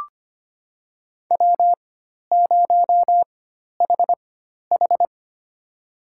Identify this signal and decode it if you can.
Morse code, heard as a keyed tone, about 25 wpm, spaced as for 12 wpm, 700 Hz. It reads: W0HH
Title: USA Call Signs - 4 Characters - 4x Character Spacing - 25wpm